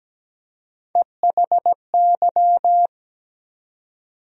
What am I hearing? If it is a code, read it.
Morse code, heard as a keyed tone, about 17 wpm, 700 Hz. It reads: EHY
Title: USA Call Sign Postfixes - 3 Letters 17wpm